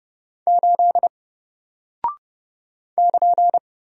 Morse code, heard as a keyed tone, about 30 wpm, 700 Hz. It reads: 8 <KN>
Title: Single Character (Letter:Number:Punctuation:Prosign) Alt 30wpm